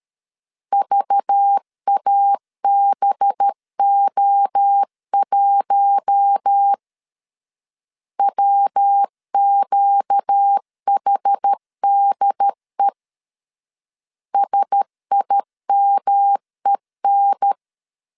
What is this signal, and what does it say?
Morse code, heard as a keyed tone, about 13 wpm, 800 Hz. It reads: VABO1 WQHDE SIMEN
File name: CW / DET / 1 6 1500C S B 12wpm 3431